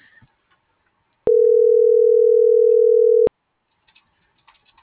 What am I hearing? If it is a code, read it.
Ringback tone (North America), 440 + 480 Hz.